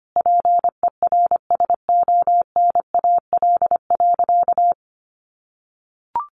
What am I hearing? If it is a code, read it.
Morse code, heard as a keyed tone, about 25 wpm, 700 Hz. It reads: PERSONAL.
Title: Sentences from Top 700 Words 25wpm